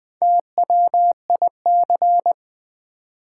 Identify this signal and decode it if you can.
Morse code, heard as a keyed tone, about 20 wpm, 700 Hz. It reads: TWIC
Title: Build A Word - 1 Letter At A Time - 5-Letter Words - 20wpm